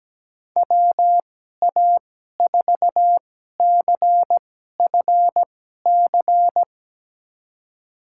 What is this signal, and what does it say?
Morse code, heard as a keyed tone, about 17 wpm, spaced as for 12 wpm, 700 Hz. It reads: WA4CFC
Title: USA Call Signs - 6 Characters - 2x Character Spacing - 17wpm